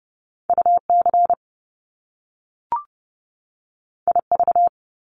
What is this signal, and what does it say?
Morse code, heard as a keyed tone, about 30 wpm, 700 Hz. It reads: UC IV
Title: Top Two Letter Combinations in Words 30wpm